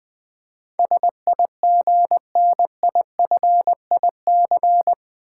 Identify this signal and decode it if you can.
Morse code, heard as a keyed tone, about 20 wpm, 700 Hz. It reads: SIGNIFIC
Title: Build A Word - 4 Letters At A Time - 11-Letter Words - 20wpm